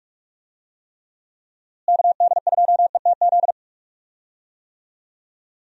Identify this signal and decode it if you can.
Morse code, heard as a keyed tone, about 46 wpm, 700 Hz. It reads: KD2ETZ
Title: USA Call Signs - 6 Characters - Repeated 3x - 45wpm